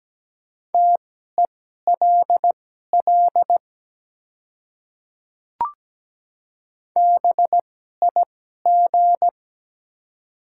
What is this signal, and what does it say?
Morse code, heard as a keyed tone, about 17 wpm, spaced as for 12 wpm, 700 Hz. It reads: TELL BIG
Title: 2x Character Spacing - Top 200 Words - 17wpm